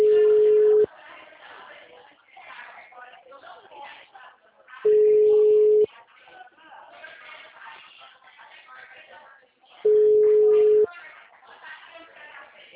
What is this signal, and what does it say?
Ringback tone (Europe, 425 Hz), with speech.